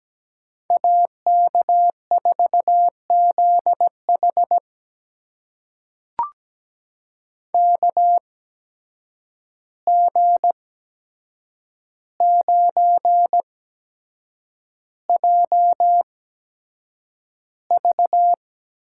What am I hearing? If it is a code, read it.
Morse code, heard as a keyed tone, about 17 wpm, spaced as for 7 wpm, 700 Hz. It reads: AK4ZH KG9JV
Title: USA Call Signs - 5 Characters - 8x Character Spacing - 17wpm